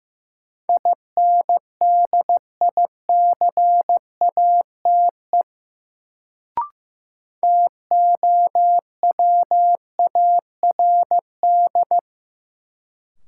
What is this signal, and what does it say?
Morse code, heard as a keyed tone, about 15 wpm, 700 Hz. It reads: INDICATE TOWARD
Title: Sentences from Top 1300 Words - New Word Drill Alt - 15wpm